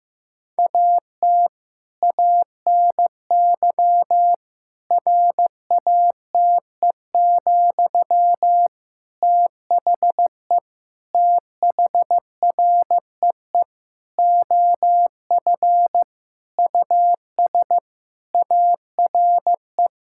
Morse code, heard as a keyed tone, about 15 wpm, 700 Hz. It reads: AT ANY RATE, THE THREE OF US ARE
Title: Sentences from Top 1000 Words - Review Part 6 - 15wpm